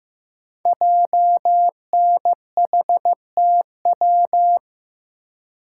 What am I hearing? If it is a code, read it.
Morse code, heard as a keyed tone, about 15 wpm, 700 Hz. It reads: JNHTW
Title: Code Groups - 5 Letters - 15wpm